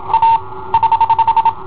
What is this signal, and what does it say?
Morse code, heard as a keyed tone, about 27 wpm, 910 Hz. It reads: A <HH>
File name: i7ohp / cassetta i7ohp B